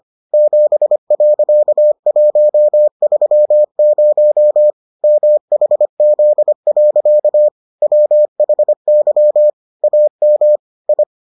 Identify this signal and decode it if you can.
Morse code, heard as a keyed tone, about 25 wpm, 600 Hz. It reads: 7.130 MHZ. WHY AM I